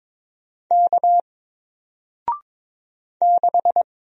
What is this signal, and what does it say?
Morse code, heard as a keyed tone, about 22 wpm, 700 Hz. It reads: K 6
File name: Single-Letter-Number-22wpm